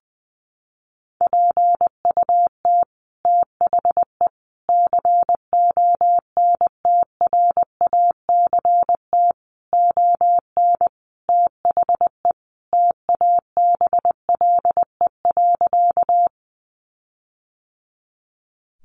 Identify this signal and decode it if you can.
Morse code, heard as a keyed tone, about 20 wpm, 700 Hz. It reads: PUT THE CONTRACT ON THE TABLE.
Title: Sentences from Top 1400 Words 20wpm